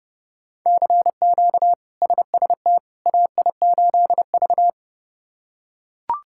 Morse code, heard as a keyed tone, about 30 wpm, 700 Hz. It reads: CQ SST AI8V
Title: K1USN SST QSO Practice - 30wpm